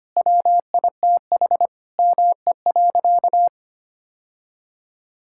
Morse code, heard as a keyed tone, about 25 wpm, 700 Hz. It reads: WITH ME.
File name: practice / 25wpm / Sentences with 4-Word Phrases - 25wpm